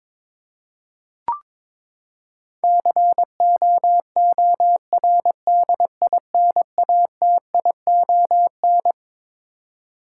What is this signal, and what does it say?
Morse code, heard as a keyed tone, about 22 wpm, 700 Hz. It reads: COORDINATION
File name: Twelve Letter Words - 22wpm